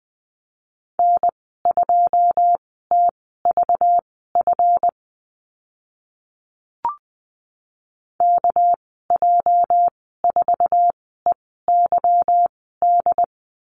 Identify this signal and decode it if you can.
Morse code, heard as a keyed tone, about 20 wpm, spaced as for 14 wpm, 700 Hz. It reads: N2TVF KJ4EYD
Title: 219 Lesson 35g - Number 0 - Call Signs 20wpm